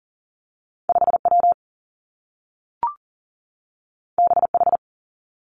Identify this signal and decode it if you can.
Morse code, heard as a keyed tone, about 39 wpm, 700 Hz. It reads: 5 W B H
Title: Two Challenging Letter-Number Combinations 40wpm